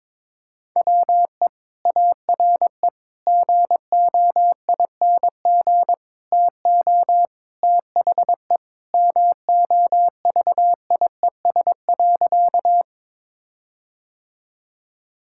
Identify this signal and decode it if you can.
Morse code, heard as a keyed tone, about 22 wpm, 700 Hz. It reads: WE ARE GOING TO THE MOVIES.
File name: practice / 22wpm / Sentences with 4-Word Phrases - 22wpm